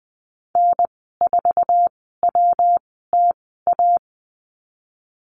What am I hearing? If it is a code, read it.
Morse code, heard as a keyed tone, about 20 wpm, spaced as for 14 wpm, 700 Hz. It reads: N4WTA
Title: 179 Lesson 29g - Slash - Call Signs 20wpm